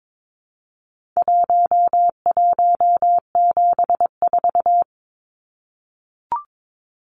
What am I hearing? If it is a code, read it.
Morse code, heard as a keyed tone, about 22 wpm, 700 Hz. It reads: 1174
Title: Numbers - Contest Style - 4-Digit 22wpm